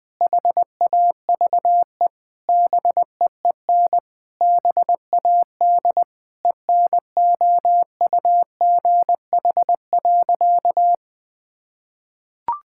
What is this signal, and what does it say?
Morse code, heard as a keyed tone, about 21 wpm, 700 Hz. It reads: HAVE BEEN BAD ENOUGH.
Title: Sentences from Top 1000 Words - Review Part 5 - 20wpm